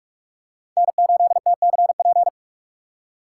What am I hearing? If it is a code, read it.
Morse code, heard as a keyed tone, about 47 wpm, 690 Hz. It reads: N8TCP
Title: USA Call Signs - 5 Characters - 4x Character Spacing - 45wpm